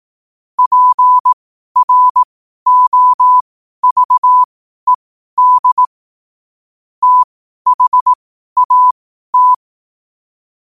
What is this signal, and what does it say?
Morse code, heard as a keyed tone, about 18 wpm, spaced as for 12 wpm, 1000 Hz. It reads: PROVED THAT
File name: qotdfiles / 12wpm